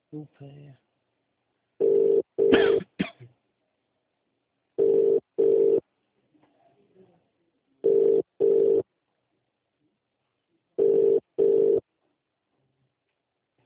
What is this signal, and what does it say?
Ringback tone (UK), 400 + 450 Hz, with speech.